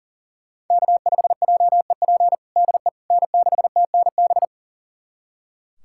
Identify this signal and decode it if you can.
Morse code, heard as a keyed tone, about 40 wpm, 700 Hz. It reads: K5JEP DE N6TNB
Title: USA Callsign DE Callsign - 5 Characters - 40wpm